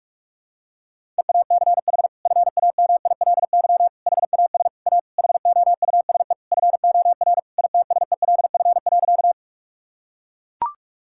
Morse code, heard as a keyed tone, about 41 wpm, 700 Hz. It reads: EACH FAMILY HAS A HOUSE FOR ITSELF.
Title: Sentences from Top 1000 Words - Review Part 5 - 45wpm